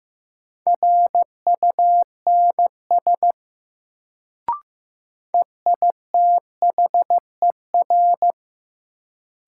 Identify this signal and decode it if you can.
Morse code, heard as a keyed tone, about 15 wpm, 700 Hz. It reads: RUNS EITHER
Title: Sentences from Top 600 Words - New Word Drill - 15wpm